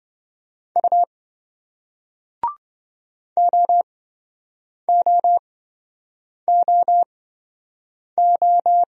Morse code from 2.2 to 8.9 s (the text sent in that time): OOOO